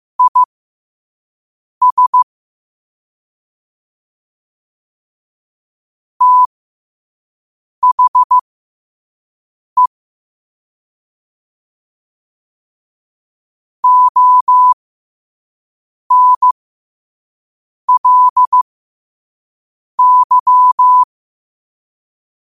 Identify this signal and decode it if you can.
Morse code, heard as a keyed tone, about 15 wpm, spaced as for 5 wpm, 1000 Hz. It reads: IS THE ONLY
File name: qotdfiles / 5wpm